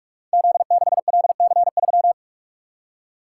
Morse code, heard as a keyed tone, about 44 wpm, 700 Hz. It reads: Z6LC3
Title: Code Groups - 5 Letter:Numbers - 45wpm